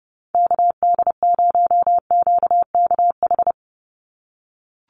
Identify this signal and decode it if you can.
Morse code, heard as a keyed tone, about 29 wpm, 700 Hz. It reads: KD0QKH
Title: USA Call Signs 30wpm